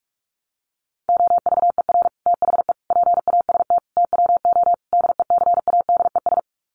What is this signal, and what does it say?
Morse code, heard as a keyed tone, about 44 wpm, 700 Hz. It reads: OVER THE PAST TWO DECADES
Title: Sets of 5 Words Encore2 45wpm